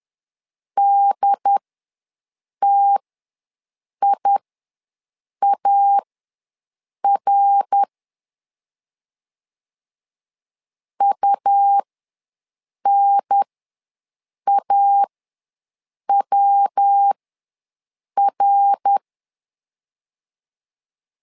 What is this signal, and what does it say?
Morse code, heard as a keyed tone, about 10 wpm, spaced as for 6 wpm, 800 Hz. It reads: DTIAR UNAWR
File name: LEZIONI / 3 MWRO / 3 MWR ws 10wpm 2151